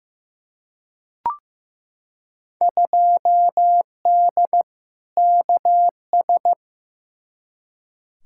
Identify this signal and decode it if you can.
Morse code, heard as a keyed tone, about 15 wpm, 700 Hz. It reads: 2D KS
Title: Field Day Exchanges Encore - 15wpm